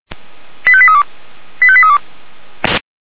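Five-tone selective call (ZVEI), heard as quick runs of five tones, tones of about 70 ms. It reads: 85832 85832
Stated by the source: ZVEI five-tone selective call 85385, received by SDR.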